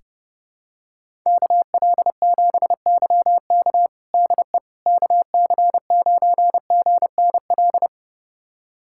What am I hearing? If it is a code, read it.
Morse code, heard as a keyed tone, about 29 wpm, 700 Hz. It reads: KL7YK DE KC9GNL
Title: USA Callsign DE Callsign 30wpm